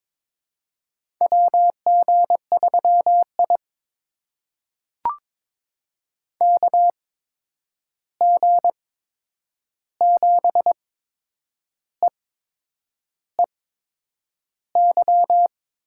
Morse code, heard as a keyed tone, about 22 wpm, spaced as for 8 wpm, 700 Hz. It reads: WG3I KG7EEY
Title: USA Call Signs - 8x Character Spacing - 22wpm